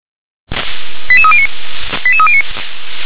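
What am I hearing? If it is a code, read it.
Five-tone selective call (ZVEI), heard as quick runs of five tones, tones of about 70 ms. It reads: 88309 88309
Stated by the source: ZVEI five-tone selective call, SDR reception.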